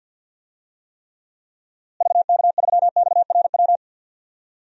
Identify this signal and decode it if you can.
Morse code, heard as a keyed tone, about 50 wpm, 700 Hz. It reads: VK3XRW
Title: DXCC-100 Call Signs - 6 Characters - Repeated 2x - 50wpm